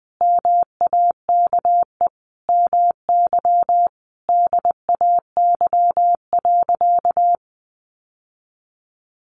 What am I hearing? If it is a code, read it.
Morse code, heard as a keyed tone, about 20 wpm, 700 Hz. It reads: MAKE MY DAY.
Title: Sentences from Top 1100 Words 20wpm